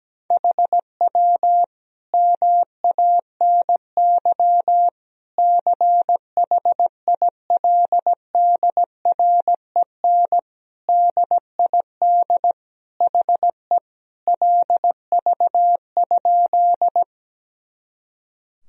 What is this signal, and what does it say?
Morse code, heard as a keyed tone, about 17 wpm, 700 Hz. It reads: HW MANY CHILDREN DID HE LV?
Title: Sentences from Top 400 Words and Abbreviations 17wpm